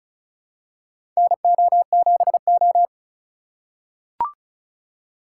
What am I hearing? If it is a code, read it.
Morse code, heard as a keyed tone, about 36 wpm, 700 Hz. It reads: NO7O
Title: USA Call Signs - 4x Character Spacing - 35wpm